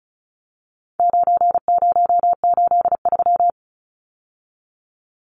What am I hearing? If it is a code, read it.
Morse code, heard as a keyed tone, about 34 wpm, 700 Hz. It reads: 9083
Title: Numbers - 4-Digit 35wpm